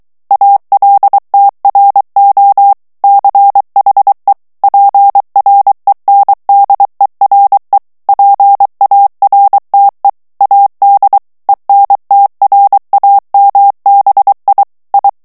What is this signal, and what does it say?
Morse code, heard as a keyed tone, about 23 wpm, 800 Hz. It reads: ALTRO CHE PRENDERE PARTE AD ENTRAMBI I
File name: ITALIEN texte clair 2 vitesse 22 mots minute